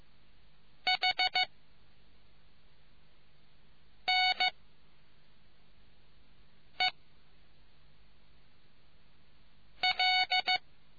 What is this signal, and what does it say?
Morse code, heard as a keyed tone, about 15 wpm, 740 Hz. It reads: H N E L